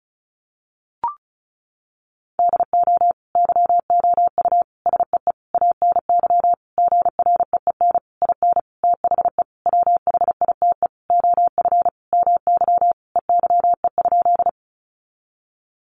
Morse code, heard as a keyed tone, about 34 wpm, 700 Hz. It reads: DO YOU SEE ANY GREEN IN THE WHITE OF MY EYE?
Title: Sentences from Top 500 Words 35wpm